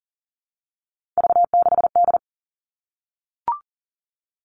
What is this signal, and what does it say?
Morse code, heard as a keyed tone, about 40 wpm, 700 Hz. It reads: V6D
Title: All Cut Numbers - 3-Digit 40wpm